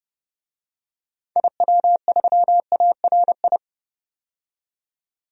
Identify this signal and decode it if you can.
Morse code, heard as a keyed tone, about 31 wpm, 700 Hz. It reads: IW3ARI
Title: DXCC-100 Call Signs - 6 Characters - 30wpm